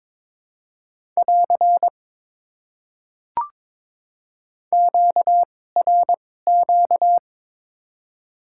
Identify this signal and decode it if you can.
Morse code, heard as a keyed tone, about 22 wpm, spaced as for 16 wpm, 700 Hz. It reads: <AR> QRQ
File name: Top 90 QSO Elements - 2x Character Spacing - 22wpm